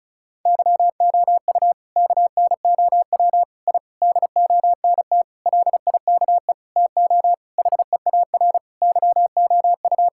Morse code, heard as a keyed tone, about 36 wpm, 700 Hz. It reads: YOU KNOW I DONT LIKE TO HEAR YOU
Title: Sentences from Top 1000 Words - Review Part 6 - 35wpm